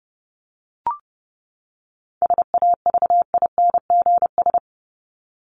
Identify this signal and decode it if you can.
Morse code, heard as a keyed tone, about 30 wpm, 700 Hz. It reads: SAVINGS
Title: Top 1001-1250 Words 30wpm